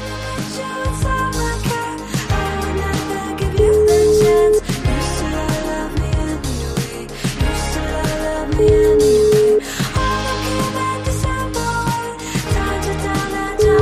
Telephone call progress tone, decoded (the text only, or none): ringback tone